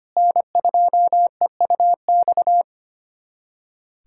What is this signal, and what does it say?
Morse code, heard as a keyed tone, about 25 wpm, 700 Hz. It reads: N2EUX
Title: USA Callsign DE Callsign - 5 Characters - 25wpm